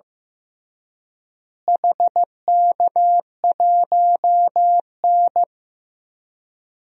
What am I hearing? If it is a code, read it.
Morse code, heard as a keyed tone, about 15 wpm, 700 Hz. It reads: HK1N
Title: DXCC-100 Call Signs - 4 Characters - Repeated 3x - 15wpm